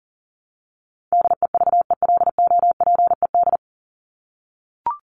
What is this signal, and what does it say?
Morse code, heard as a keyed tone, about 37 wpm, 700 Hz. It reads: DEVELOPED
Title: ED Suffix - Top 100 Words with Base Words 40wpm